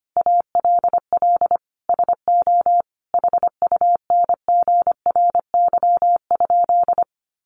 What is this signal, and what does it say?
Morse code, heard as a keyed tone, about 25 wpm, 700 Hz. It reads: ALL SO HUNGRY?
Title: Sentences from Top 1400 Words 25wpm